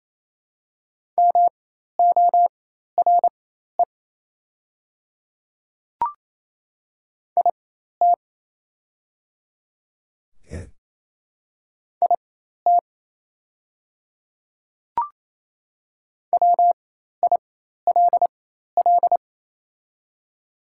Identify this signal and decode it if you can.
Morse code, heard as a keyed tone, about 28 wpm, spaced as for 13 wpm, 700 Hz. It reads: MORE IT IT WILL
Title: 4x Character Spacing - Top 100 Words - 28wpm